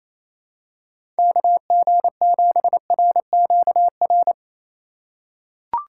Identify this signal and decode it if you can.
Morse code, heard as a keyed tone, about 28 wpm, 700 Hz. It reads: KG7RQR
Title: USA Call Signs - 6 Characters - 3x Character Spacing - 28wpm